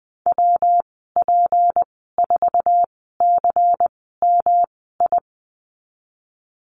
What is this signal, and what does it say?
Morse code, heard as a keyed tone, about 20 wpm, spaced as for 14 wpm, 700 Hz. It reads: WP4CMI
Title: 207 Lesson 33g - Letter j - Call Signs 20wpm